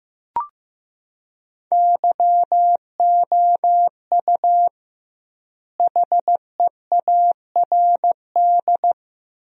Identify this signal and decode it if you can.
Morse code, heard as a keyed tone, about 15 wpm, 700 Hz. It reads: YOU HEARD
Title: 2x Word-Spacing - Sentences from Top 500 Words - 15wpm